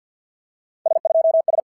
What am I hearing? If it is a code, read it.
Morse code, heard as a keyed tone, about 50 wpm, 650 Hz. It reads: S2S
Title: SOTA QSO Practice 50wpm